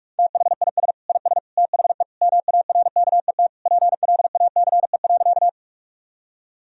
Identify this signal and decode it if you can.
Morse code, heard as a keyed tone, about 41 wpm, 700 Hz. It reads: THIS IS THE MARKET PLACE.
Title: Sentences from Top 1000 Words - Review Part 3 - 45wpm